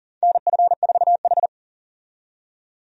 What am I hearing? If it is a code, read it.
Morse code, heard as a keyed tone, about 40 wpm, 700 Hz. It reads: NF4H
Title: USA Call Signs - 4 Characters - 40wpm